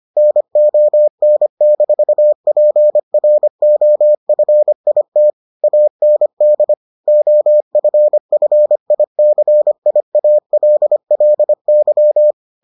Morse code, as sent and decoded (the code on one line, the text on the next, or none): -. --- -. -....- .--. .-. --- ..-. .. - / .- -. -.. / --- ..-. ..-. .. -.-. .. .- .-.. .-.. -.--
NON-PROFIT AND OFFICIALLY